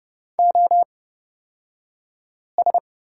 Morse code, heard as a keyed tone, about 30 wpm, spaced as for 5 wpm, 700 Hz. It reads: OS
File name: Lesson 36d - BT - Two Characters 30wpm